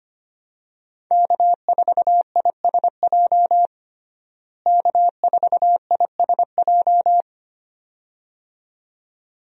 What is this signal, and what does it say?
Morse code, heard as a keyed tone, about 25 wpm, 700 Hz. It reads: K4ISJ K4ISJ
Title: USA Call Signs - 5 Characters - Repeated 2x - 25wpm